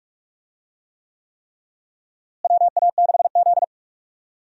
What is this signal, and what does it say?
Morse code, heard as a keyed tone, about 42 wpm, 700 Hz. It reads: WA6Z
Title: USA Call Signs - 4 Characters - Repeated 3x - 45wpm